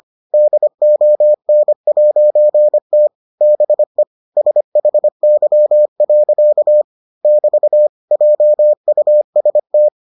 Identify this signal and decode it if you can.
Morse code, heard as a keyed tone, about 25 wpm, 600 Hz. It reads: DON'T BE SHY. <BT> JUST